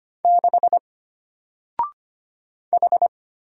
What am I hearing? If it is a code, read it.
Morse code, heard as a keyed tone, about 25 wpm, 700 Hz. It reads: 6 H
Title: Single Letter-Number Alt 25wpm